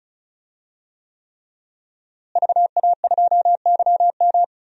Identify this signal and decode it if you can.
Morse code, heard as a keyed tone, about 36 wpm, 700 Hz. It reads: VA2YM
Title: DXCC-100 Call Signs - 5 Characters - Repeated 3x - 35wpm